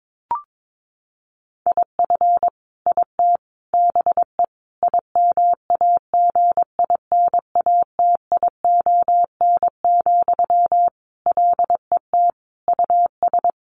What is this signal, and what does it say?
Morse code, heard as a keyed tone, about 22 wpm, 700 Hz. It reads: IF IT BE IMAGINATION, LET US
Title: Sentences from Top 1300 Words 22wpm